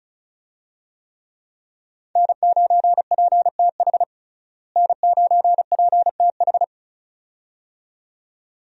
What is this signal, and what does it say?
Morse code, heard as a keyed tone, about 36 wpm, 700 Hz. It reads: N9PTH N9PTH
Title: USA Call Signs - 5 Characters - Repeated 2x - 35wpm